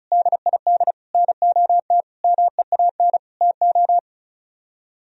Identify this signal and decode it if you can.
Morse code, heard as a keyed tone, about 45 wpm, 700 Hz. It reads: DID NOT MEAN TO